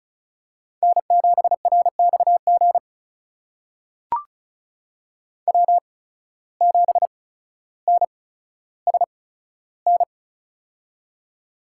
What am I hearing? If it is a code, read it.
Morse code, heard as a keyed tone, about 34 wpm, spaced as for 15 wpm, 700 Hz. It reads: N7RXG W7NSN
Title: USA Call Signs - 5 Characters - 8x Character Spacing - 35wpm